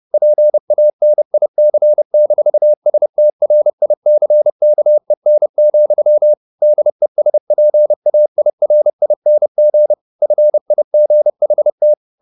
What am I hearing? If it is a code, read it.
Morse code, heard as a keyed tone, about 30 wpm, 600 Hz. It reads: PANIC-STRICKEN, DESPAIRING FIGHT
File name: the old man and the sea / the old man and the sea 1 30wpm